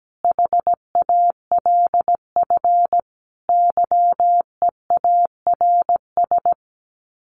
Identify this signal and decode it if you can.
Morse code, heard as a keyed tone, about 17 wpm, 700 Hz. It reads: HALF YEARS
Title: Sets of 5 Words Encore2 17wpm